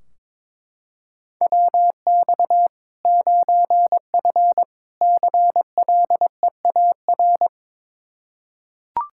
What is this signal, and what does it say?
Morse code, heard as a keyed tone, about 22 wpm, 700 Hz. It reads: WX 9F CLEAR
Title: K1USN SST QSO Practice 22wpm